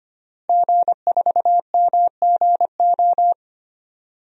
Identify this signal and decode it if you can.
Morse code, heard as a keyed tone, about 25 wpm, 700 Hz. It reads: G4MGO
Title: DXCC-100 Call Signs - 5 Characters - Repeated 2x - 25wpm